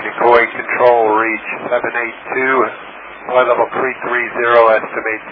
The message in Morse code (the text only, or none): none